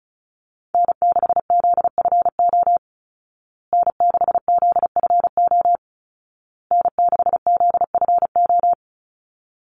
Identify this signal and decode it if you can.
Morse code, heard as a keyed tone, about 34 wpm, 700 Hz. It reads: N6ZFO N6ZFO N6ZFO
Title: Call Signs Repeated 2x - 35wpm